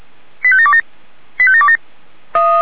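Five-tone selective call (ZVEI), heard as quick runs of five tones, tones of about 70 ms. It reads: 85727 85727
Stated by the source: ZVEI five-tone selective call received on an SDR.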